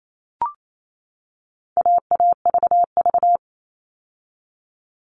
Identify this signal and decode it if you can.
Morse code, heard as a keyed tone, about 28 wpm, 700 Hz. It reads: AAVV
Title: All Cut Numbers - 4-Digit 28wpm